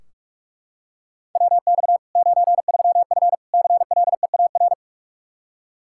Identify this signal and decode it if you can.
Morse code, heard as a keyed tone, about 41 wpm, 700 Hz. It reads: WX 93F CLEAR
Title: Practice QSO - Exchange 2 - Info - 45wpm